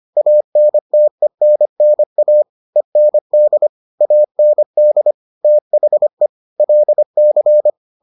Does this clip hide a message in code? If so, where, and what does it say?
Yes: Morse code from 0.2 to 7.7 s, heard as a keyed tone, about 25 wpm, 600 Hz. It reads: ANTENNA END AND THE LC